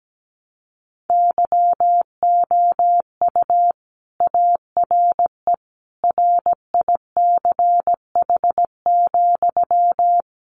Morse code, heard as a keyed tone, about 17 wpm, 700 Hz. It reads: YOU ARE RICH,
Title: Sentences from Top 1100 Words 17wpm